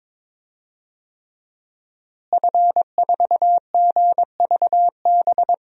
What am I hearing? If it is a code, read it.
Morse code, heard as a keyed tone, about 22 wpm, 700 Hz. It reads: F4GVB
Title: DXCC-100 Call Signs - 5 Characters - Repeated 3x - 22wpm